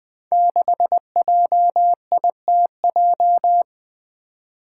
Morse code, heard as a keyed tone, about 20 wpm, 700 Hz. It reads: 6JITJ
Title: Five Challenging Letter-Number Combinations 20wpm